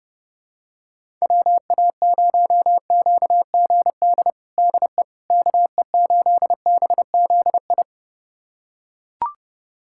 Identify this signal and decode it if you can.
Morse code, heard as a keyed tone, about 32 wpm, 700 Hz. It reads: WA0QGD DE KE8BZI
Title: USA Callsign DE Callsign 30wpm